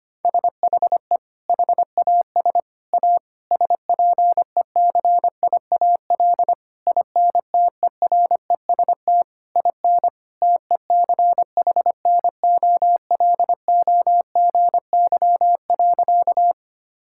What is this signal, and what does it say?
Morse code, heard as a keyed tone, about 25 wpm, 700 Hz. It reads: SHE HAS A SPECIAL INTEREST IN TECHNOLOGY.
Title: Sentences from Top 1000 Words - Review Part 5 - 25wpm